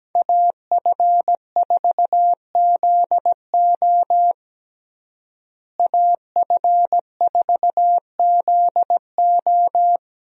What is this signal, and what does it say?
Morse code, heard as a keyed tone, about 17 wpm, 700 Hz. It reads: AF4ZO AF4ZO
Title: USA Call Signs - 5 Characters - Repeated 3x - 17wpm